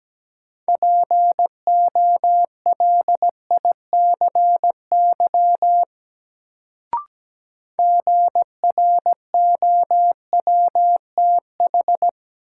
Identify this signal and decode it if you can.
Morse code, heard as a keyed tone, about 17 wpm, 700 Hz. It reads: POLICY GROWTH
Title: Sentences from Top 1000 Words - Review Part 5 - New Word Drill - 17wpm